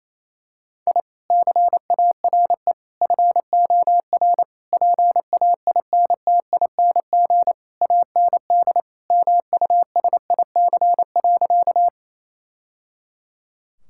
Morse code, heard as a keyed tone, about 28 wpm, 700 Hz. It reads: I CARE FOR PAINTING AND MUSIC.
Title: Sentences from Top 1000 Words - Review Part 6 - 28wpm